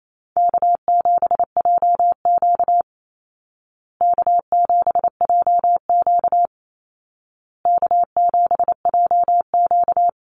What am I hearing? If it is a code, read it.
Morse code, heard as a keyed tone, about 28 wpm, 700 Hz. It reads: K7JQ K7JQ K7JQ
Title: Easy Call Signs Repeated 3x - 28wpm